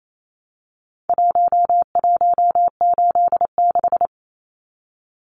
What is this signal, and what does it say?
Morse code, heard as a keyed tone, about 28 wpm, 700 Hz. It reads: 1186
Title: Numbers - Contest Style - 4-Digit 28wpm